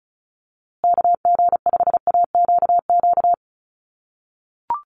Morse code, heard as a keyed tone, about 34 wpm, 700 Hz. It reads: KG5AQQ
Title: USA Call Signs Repeated 3x 35wpm